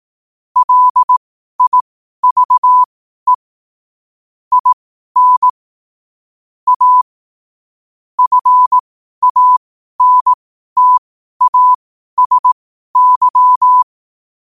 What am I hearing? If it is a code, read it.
Morse code, heard as a keyed tone, about 18 wpm, spaced as for 12 wpm, 1000 Hz. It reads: LIVE IN A FANTASY
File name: qotdfiles / 12wpm